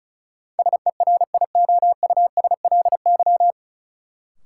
Morse code, heard as a keyed tone, about 36 wpm, 700 Hz. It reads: SERIOUSLY
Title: Sentences from Top 1200 Words - New Word Drill Alt - 35wpm